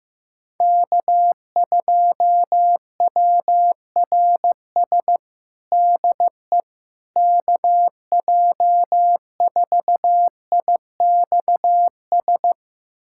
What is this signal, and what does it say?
Morse code, heard as a keyed tone, about 15 wpm, 700 Hz. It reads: K2WRS DE KJ4IXS
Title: USA Callsign DE Callsign 15wpm